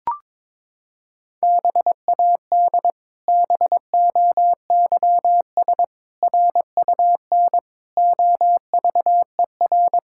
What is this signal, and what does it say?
Morse code, heard as a keyed tone, about 22 wpm, 700 Hz. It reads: BAD BOYS RUN OVER
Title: Pangram Challenge v2 - 22wpm